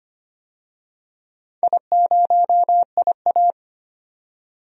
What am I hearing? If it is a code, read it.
Morse code, heard as a keyed tone, about 25 wpm, 700 Hz. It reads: I0IA
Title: DXCC-100 Call Signs - 4 Characters - 25wpm